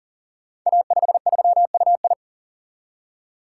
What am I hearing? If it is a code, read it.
Morse code, heard as a keyed tone, about 39 wpm, 700 Hz. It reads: A53UI
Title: Code Groups - 5 Letter:Numbers - 40wpm